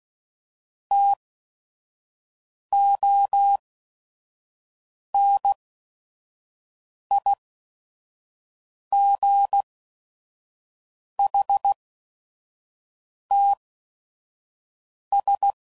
Morse code, heard as a keyed tone, about 16 wpm, spaced as for 5 wpm, 800 Hz. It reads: TONIGHTS
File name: CW Code Groups @5WPM / Weather Sets / Weather 4 24